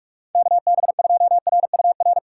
Morse code, heard as a keyed tone, about 44 wpm, 700 Hz. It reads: KB2RUR
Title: USA Call Signs - 6 Characters - 4x Character Spacing - 45wpm